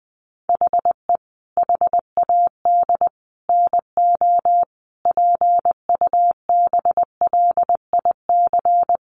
Morse code, heard as a keyed tone, about 20 wpm, 700 Hz. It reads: HE HAD NO PUBLIC
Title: Sentences from Top 1500 Words 20wpm